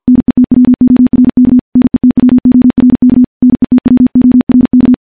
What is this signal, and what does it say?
Morse code, heard as a keyed tone, about 51 wpm, 250 Hz. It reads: NETWORK NETWORK NETWORK
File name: morse / morse-training-50WPM